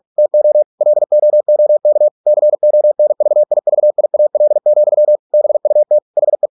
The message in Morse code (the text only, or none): TO LOOK CONVIVIAL, BUT HE